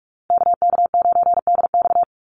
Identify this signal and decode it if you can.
Morse code, heard as a keyed tone, about 44 wpm, 700 Hz. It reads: KK9DX
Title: Call Signs Repeated 2x - 45wpm